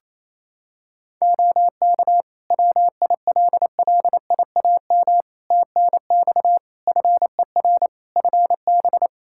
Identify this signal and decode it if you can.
Morse code, heard as a keyed tone, about 28 wpm, 700 Hz. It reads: OK WILLIAM TNX FER FB
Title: Practice QSO - Exchange 4 - Ending - 28wpm